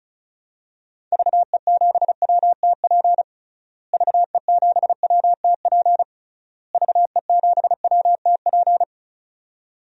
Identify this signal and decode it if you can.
Morse code, heard as a keyed tone, about 36 wpm, 700 Hz. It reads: VE7WTP VE7WTP VE7WTP
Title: DXCC-100 Call Signs - 6 Characters - Repeated 3x - 35wpm